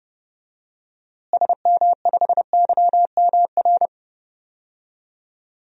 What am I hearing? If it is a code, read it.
Morse code, heard as a keyed tone, about 30 wpm, 700 Hz. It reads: SM5YMR
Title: DXCC-100 Call Signs - 6 Characters - Repeated 3x - 30wpm